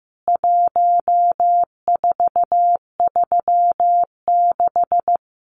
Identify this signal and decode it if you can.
Morse code, heard as a keyed tone, about 15 wpm, 700 Hz. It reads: 1436
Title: CWT Exchange 15wpm